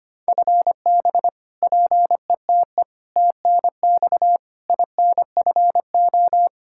Morse code, heard as a keyed tone, about 25 wpm, 700 Hz. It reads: FB PETE TNX INFO
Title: Practice QSO - Exchange 3 - More Info - 25wpm